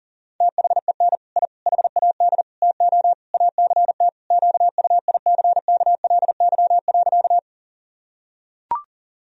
Morse code, heard as a keyed tone, about 40 wpm, 700 Hz. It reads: THEN I HAD TO ACT QUICKLY.
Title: Sentences from Top 1000 Words - Review Part 1 - 40wpm